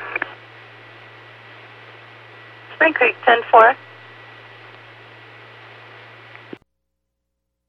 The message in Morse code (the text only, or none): none